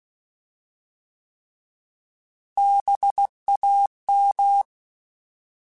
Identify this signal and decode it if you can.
Morse code, heard as a keyed tone, about 16 wpm, 800 Hz. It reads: BAM
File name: Alphabetical / Alphabetical mp3s / Bs @15